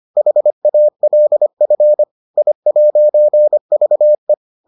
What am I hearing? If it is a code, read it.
Morse code, heard as a keyed tone, about 25 wpm, 600 Hz. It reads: HALF I'VE